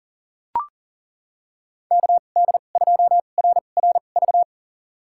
Morse code, heard as a keyed tone, about 40 wpm, spaced as for 29 wpm, 700 Hz. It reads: KD2RRV
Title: USA Call Signs - 2x Character Spacing - 40wpm